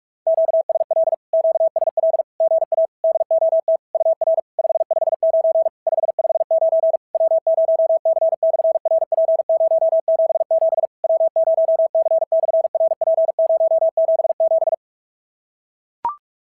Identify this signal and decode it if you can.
Morse code, heard as a keyed tone, about 44 wpm, 660 Hz. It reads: QSL QSL GA DOT UR 559 559 W0C/RP077 W0C/RP077
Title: SOTA QSO Practice 45wpm